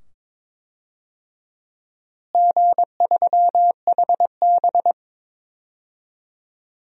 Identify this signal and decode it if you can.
Morse code, heard as a keyed tone, about 22 wpm, 700 Hz. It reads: G3HB
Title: DXCC-100 Call Signs - 4 Characters - Repeated 2x - 22wpm